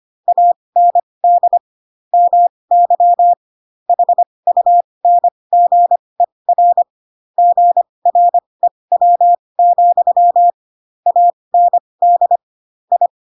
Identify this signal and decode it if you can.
Morse code, heard as a keyed tone, about 25 wpm, spaced as for 20 wpm, 700 Hz. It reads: AND MY HUNGER GREW, AND I
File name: Pinocchio / Pinocchio ch7 15 25 ik1hge